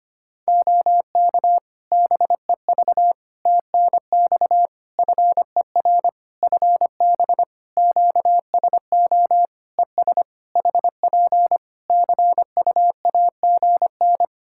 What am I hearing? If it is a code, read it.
Morse code, heard as a keyed tone, about 25 wpm, 700 Hz. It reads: OK BEV TNX FER FB QSO ES HP CUAGN
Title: Practice QSO - Exchange 4 - Ending - 25wpm